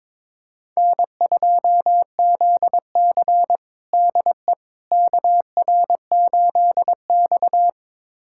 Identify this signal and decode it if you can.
Morse code, heard as a keyed tone, about 22 wpm, 700 Hz. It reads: N2ZC DE KR8X
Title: USA Callsign DE Callsign - 4 Characters - 22wpm